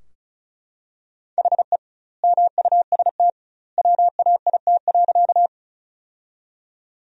Morse code, heard as a keyed tone, about 36 wpm, 700 Hz. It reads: HE MUST WAIT.
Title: 2x Word-Spacing - Sentences from Top 500 Words - 35wpm